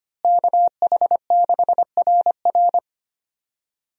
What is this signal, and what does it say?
Morse code, heard as a keyed tone, about 25 wpm, 700 Hz. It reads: KH6RR
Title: USA Call Signs - 5 Characters - Repeated 2x - 25wpm